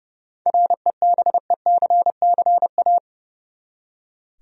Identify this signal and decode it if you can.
Morse code, heard as a keyed tone, about 30 wpm, 700 Hz. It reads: REBECCA
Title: USA Common Names - 30wpm